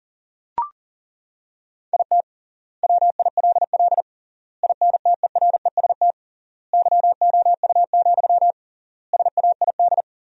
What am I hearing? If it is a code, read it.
Morse code, heard as a keyed tone, about 40 wpm, 700 Hz. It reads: IT WILL INTEREST YOU, SAID
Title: 3x Word-Spacing - Sentences from Top 400 Words - 40wpm